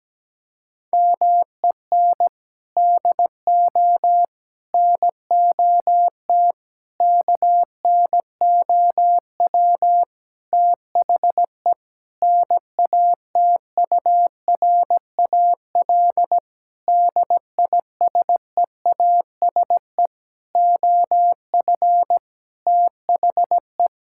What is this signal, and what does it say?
Morse code, heard as a keyed tone, about 17 wpm, 700 Hz. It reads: MEN DO NOT KNOW THE NATURAL DISEASE OF THE
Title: Sentences from Top 1000 Words - Review Part 6 - 17wpm